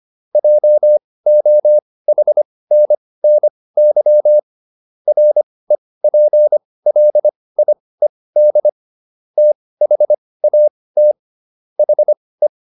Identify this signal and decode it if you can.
Morse code, heard as a keyed tone, about 25 wpm, spaced as for 18 wpm, 600 Hz. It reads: JOHNNY REPLIED THAT HE